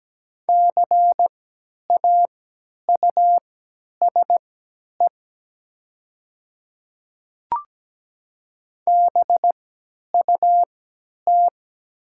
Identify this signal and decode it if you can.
Morse code, heard as a keyed tone, about 17 wpm, spaced as for 10 wpm, 700 Hz. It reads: CAUSE BUT
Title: 3x Character Spacing - Top 200 Words - 17wpm